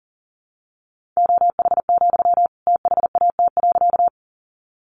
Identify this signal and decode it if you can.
Morse code, heard as a keyed tone, about 40 wpm, 700 Hz. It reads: OH, THAT.
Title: Sentences from Top 500 Words 40wpm